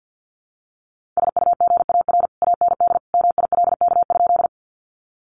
Morse code, heard as a keyed tone, about 50 wpm, 700 Hz. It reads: SUGAR AND MILK?
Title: Sentences from Top 1500 Words 50wpm